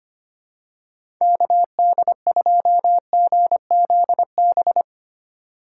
Morse code, heard as a keyed tone, about 25 wpm, 700 Hz. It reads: KD2GZB
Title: USA Call Signs - 6 Characters - 8x Character Spacing - 25wpm